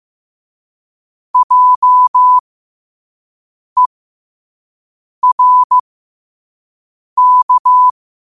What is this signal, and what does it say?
Morse code, heard as a keyed tone, about 15 wpm, spaced as for 5 wpm, 1000 Hz. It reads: JERK
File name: qotdfiles / 5wpm